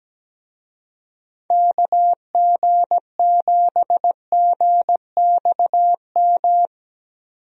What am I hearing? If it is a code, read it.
Morse code, heard as a keyed tone, about 17 wpm, 700 Hz. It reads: KG7GXM
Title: USA Call Signs - 6 Characters - 17wpm